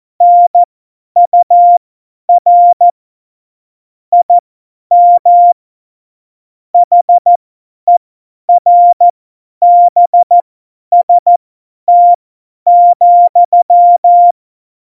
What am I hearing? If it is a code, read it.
Morse code, heard as a keyed tone, about 14 wpm, spaced as for 10 wpm, 700 Hz. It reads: NUR IM HERBST,
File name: Morsekurs / Witze / Witz 21. Mai 2019, 50 F70